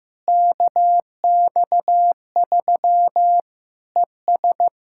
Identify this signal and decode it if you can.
Morse code, heard as a keyed tone, about 15 wpm, 700 Hz. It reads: KX3 ES
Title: Practice QSO - Exchange 2 - Info - 15wpm